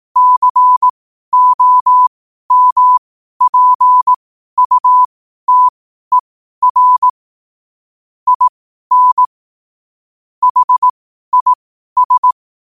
Morse code, heard as a keyed tone, about 18 wpm, spaced as for 12 wpm, 1000 Hz. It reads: COMPUTER IN HIS
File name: qotdfiles / 12wpm